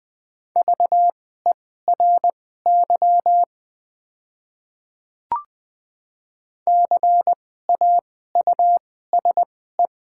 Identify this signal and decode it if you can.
Morse code, heard as a keyed tone, about 21 wpm, spaced as for 15 wpm, 700 Hz. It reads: VERY CAUSE
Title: 2x Character Spacing - Top 300 Words - 20wpm